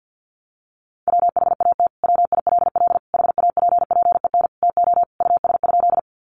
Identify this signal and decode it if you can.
Morse code, heard as a keyed tone, about 50 wpm, 700 Hz. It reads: WHAT WILL HAPPEN TO US?
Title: Sentences from Top 400 Words 50wpm